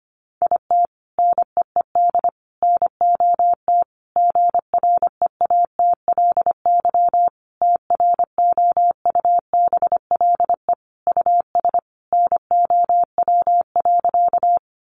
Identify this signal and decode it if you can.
Morse code, heard as a keyed tone, about 25 wpm, 700 Hz. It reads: IT NEED NOT GREATLY TROUBLE US NOW.
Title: Sentences from Top 1400 Words 25wpm